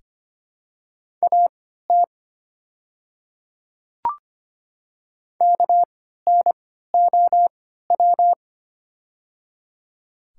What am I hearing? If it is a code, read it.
Morse code, heard as a keyed tone, about 25 wpm, spaced as for 14 wpm, 700 Hz. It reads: AT KNOW
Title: 3x Character Spacing - Top 300 Words - 25wpm